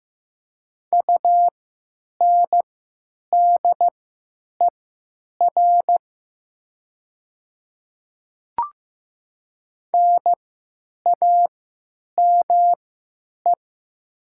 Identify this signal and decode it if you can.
Morse code, heard as a keyed tone, about 15 wpm, spaced as for 9 wpm, 700 Hz. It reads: UNDER NAME
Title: 3x Character Spacing - Top 500 Words - 15wpm